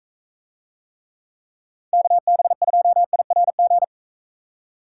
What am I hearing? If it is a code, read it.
Morse code, heard as a keyed tone, about 42 wpm, 700 Hz. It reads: KB2IRG